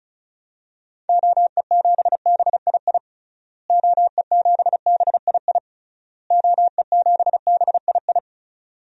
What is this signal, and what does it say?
Morse code, heard as a keyed tone, about 36 wpm, 700 Hz. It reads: OE7BII OE7BII OE7BII
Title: DXCC-100 Call Signs - 6 Characters - Repeated 3x - 35wpm